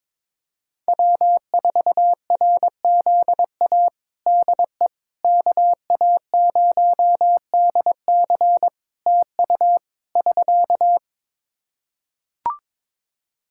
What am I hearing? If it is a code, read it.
Morse code, heard as a keyed tone, about 22 wpm, 700 Hz. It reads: W4RZA DE KA0DC TU <SK>
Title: Practice QSO - Exchange 4 - Ending - 22wpm